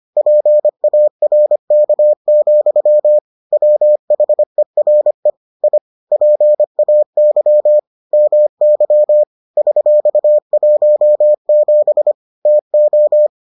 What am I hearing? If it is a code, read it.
Morse code, heard as a keyed tone, about 25 wpm, 600 Hz. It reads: PARK, WHERE I PAY MY $17 TO